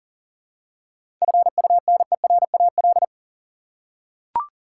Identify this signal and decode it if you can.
Morse code, heard as a keyed tone, about 40 wpm, 700 Hz. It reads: FUNERAL
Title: Seven Letter Words - 40wpm